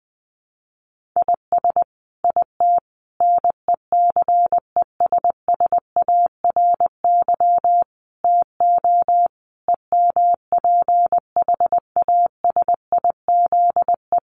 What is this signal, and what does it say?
Morse code, heard as a keyed tone, about 20 wpm, 700 Hz. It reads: IS IT NECESSARY TO EMPHASIZE
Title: Sentences from Top 1500 Words 20wpm